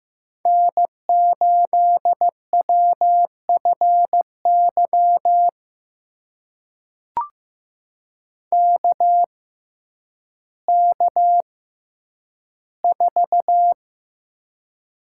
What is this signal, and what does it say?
Morse code, heard as a keyed tone, about 15 wpm, spaced as for 9 wpm, 700 Hz. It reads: N8WFY KK4
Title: USA Call Signs - 5 Characters - 6x Character Spacing - 15wpm